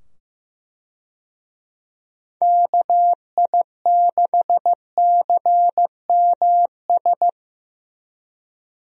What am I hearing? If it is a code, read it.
Morse code, heard as a keyed tone, about 16 wpm, 700 Hz. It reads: KI6CMS
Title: USA Call Signs - 6 Characters - 15wpm